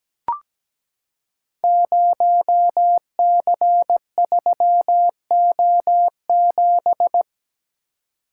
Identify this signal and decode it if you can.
Morse code, heard as a keyed tone, about 17 wpm, 700 Hz. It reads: 0C3O7
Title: Code Groups - 5 Letter:Numbers - 17wpm